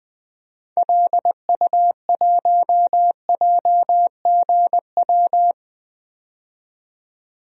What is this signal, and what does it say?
Morse code, heard as a keyed tone, about 20 wpm, 700 Hz. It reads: LU1JGW
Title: DXCC-100 Call Signs - 6 Characters - 20wpm